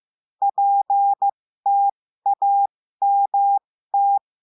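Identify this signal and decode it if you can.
Morse code, heard as a keyed tone, about 15 wpm, spaced as for 13 wpm, 800 Hz. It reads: PTAMT
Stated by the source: Morse lesson adding the letter T, with five-letter practice groups.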